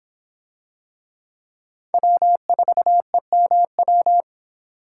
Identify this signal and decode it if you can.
Morse code, heard as a keyed tone, about 25 wpm, 700 Hz. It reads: W4EMW